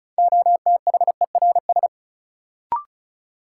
Morse code, heard as a keyed tone, about 36 wpm, 700 Hz. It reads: OTHERS
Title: Sentences from Top 1000 Words - Review Part 2 - New Word Drill - 35wpm